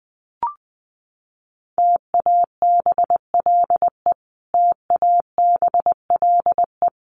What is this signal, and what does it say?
Morse code, heard as a keyed tone, about 20 wpm, 700 Hz. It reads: TABLE TABLE
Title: Top 500 Words Repeated 20wpm